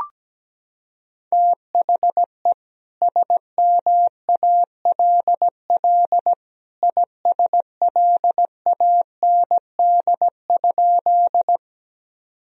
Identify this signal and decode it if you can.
Morse code, heard as a keyed tone, about 17 wpm, 700 Hz. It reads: THE SMALL ISLAND?
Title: Sentences from Top 1000 Words - Review Part 3 - 17wpm